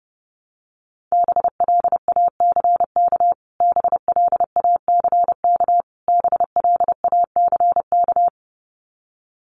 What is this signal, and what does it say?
Morse code, heard as a keyed tone, about 28 wpm, 700 Hz. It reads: BLACK BLACK BLACK
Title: Top 500 Words Repeated 30wpm